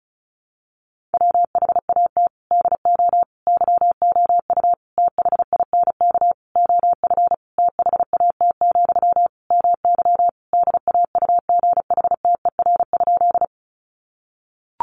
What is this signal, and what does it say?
Morse code, heard as a keyed tone, about 34 wpm, 700 Hz. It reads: WHAT DO YOU THINK OF THAT, MY DAUGHTER?
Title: Sentences from Top 1500 Words 35wpm